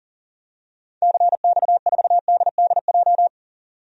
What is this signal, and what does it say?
Morse code, heard as a keyed tone, about 40 wpm, 700 Hz. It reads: CX4DDJ
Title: DXCC-100 Call Signs - 6 Characters - 40wpm